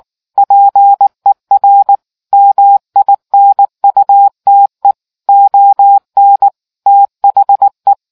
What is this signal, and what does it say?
Morse code, heard as a keyed tone, about 19 wpm, 800 Hz. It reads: PER MINUTE ON THE